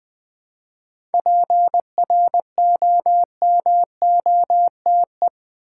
Morse code, heard as a keyed tone, about 20 wpm, 700 Hz. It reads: PROMOTE
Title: 7 Letter Words - First 3 Letters Given - 20wpm